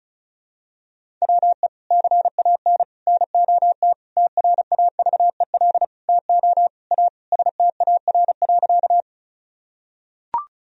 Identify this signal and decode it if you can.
Morse code, heard as a keyed tone, about 36 wpm, 700 Hz. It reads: WE CAN NOT TRAVEL TO A STAR.
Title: Sentences from Top 1000 Words - Review Part 2 - 35wpm